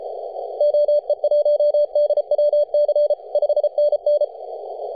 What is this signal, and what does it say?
Morse code, heard as a keyed tone, about 33 wpm, 590 Hz. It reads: OE1DWC 5NN
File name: posts / 2015-july-18-0917z-s79hn-21-035mhz-cw